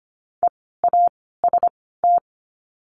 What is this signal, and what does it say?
Morse code, heard as a keyed tone, about 25 wpm, spaced as for 16 wpm, 700 Hz. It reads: EAST
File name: course / 25:10wpm / 094 Lesson 16f - Letter c - Words 25wpm